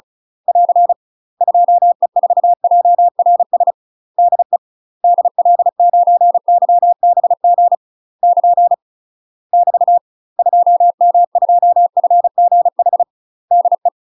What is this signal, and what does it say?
Morse code, heard as a keyed tone, about 36 wpm, 700 Hz. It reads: <AR> 2E4JRS DE DL9YBG <KN> <BT> 2M2FGH DE
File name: UK MASTER QSOs TRAINING/35WPM